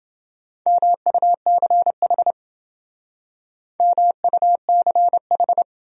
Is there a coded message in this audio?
Yes: Morse code from 0.7 to 5.6 s, heard as a keyed tone, about 28 wpm, 700 Hz. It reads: MUCH MUCH